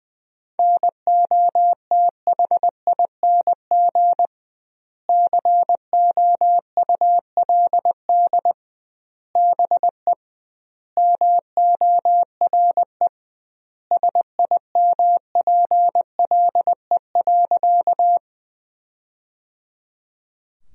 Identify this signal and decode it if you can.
Morse code, heard as a keyed tone, about 20 wpm, 700 Hz. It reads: NOTHING COULD BE MORE SIMPLE.
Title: 2x Word-Spacing - Sentences from Top 500 Words - 20wpm